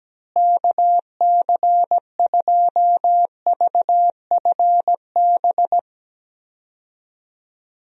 Morse code, heard as a keyed tone, about 17 wpm, 700 Hz. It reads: KC2VFB
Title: USA Call Signs - 6 Characters - Repeated 2x - 17wpm